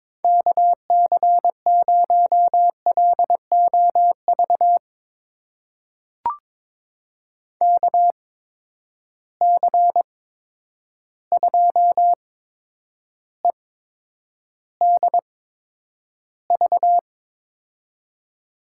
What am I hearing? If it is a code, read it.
Morse code, heard as a keyed tone, about 21 wpm, spaced as for 9 wpm, 700 Hz. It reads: KC0LOV KC2EDV
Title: USA Call Signs - 6 Characters - 8x Character Spacing - 22wpm